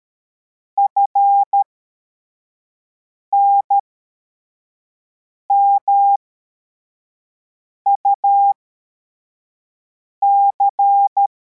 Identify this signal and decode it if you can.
Morse code, heard as a keyed tone, about 13 wpm, spaced as for 4 wpm, 800 Hz. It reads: FNMUC
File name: formation / cw / Text64